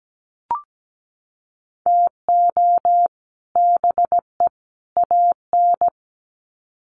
Morse code, heard as a keyed tone, about 17 wpm, 700 Hz. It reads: TO BE AN
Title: Sets of 3 Words 17wpm